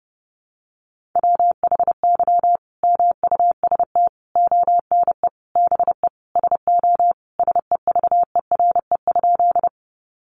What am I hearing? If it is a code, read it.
Morse code, heard as a keyed tone, about 28 wpm, 700 Hz. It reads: WHY MUST ONE BE SO SEVERE?
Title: Sentences from Top 1400 Words 30wpm